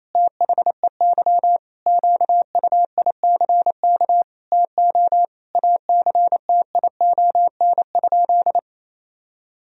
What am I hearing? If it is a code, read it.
Morse code, heard as a keyed tone, about 28 wpm, 700 Hz. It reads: THEY QUICK TO ACTION?
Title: Sentences from Top 1000 Words - Review Part 4 - 28wpm